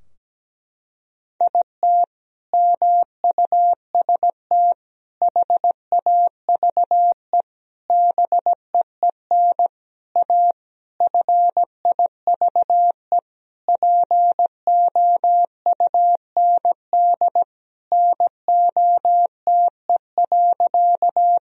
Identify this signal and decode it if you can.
Morse code, heard as a keyed tone, about 17 wpm, 700 Hz. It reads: IT MUST HAVE BEEN A FIVE POUND NOTE.
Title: Sentences from Top 1000 Words - Review Part 2 - 17wpm